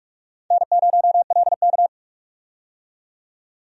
Morse code, heard as a keyed tone, about 47 wpm, 700 Hz. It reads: N0LK
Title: USA Call Signs - 4 Characters - 45wpm